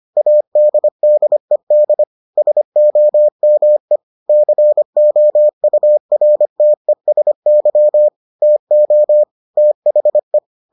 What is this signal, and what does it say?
Morse code, heard as a keyed tone, about 25 wpm, 600 Hz. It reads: ADDED SOME COURTESY TO THE